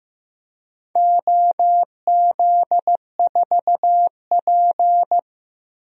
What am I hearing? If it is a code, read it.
Morse code, heard as a keyed tone, about 15 wpm, 700 Hz. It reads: OZ4P
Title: DXCC-100 Call Signs - 4 Characters - 15wpm